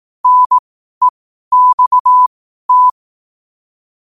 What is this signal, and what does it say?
Morse code, heard as a keyed tone, about 18 wpm, spaced as for 12 wpm, 1000 Hz. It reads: NEXT
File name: qotdfiles / 12wpm